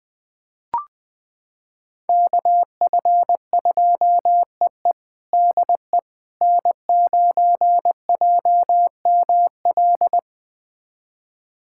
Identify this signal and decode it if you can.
Morse code, heard as a keyed tone, about 20 wpm, 700 Hz. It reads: KF2EE DE N9JML
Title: USA Callsign DE Callsign - 5 Characters - 20wpm